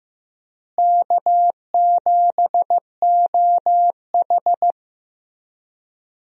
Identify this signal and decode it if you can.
Morse code, heard as a keyed tone, about 15 wpm, 700 Hz. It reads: K7OH
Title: USA Call Signs - 4 Characters - Repeated 3x - 15wpm